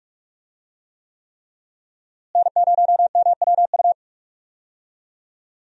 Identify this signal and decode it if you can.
Morse code, heard as a keyed tone, about 46 wpm, 700 Hz. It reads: N0MWU
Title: USA Call Signs - 5 Characters - Repeated 3x - 45wpm